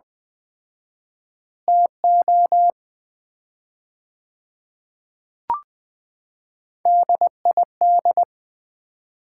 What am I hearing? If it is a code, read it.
Morse code, heard as a keyed tone, about 20 wpm, 700 Hz. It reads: TO DID
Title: 4x Word-Spacing - Sets of 5 Words - 20wpm